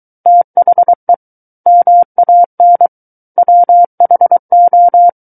Morse code, heard as a keyed tone, about 23 wpm, 700 Hz. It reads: THE MAN WHO
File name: Book One / 14-Chapter 14 - 23 WPM